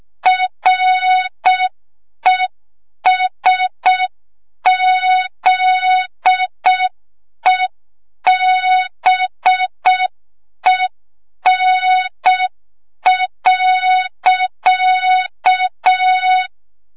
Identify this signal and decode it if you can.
Morse code, heard as a keyed tone, about 6 wpm, 3700 Hz. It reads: RESZEBEN.
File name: Morze-HA2MN / hang / 25 GYAKORLAT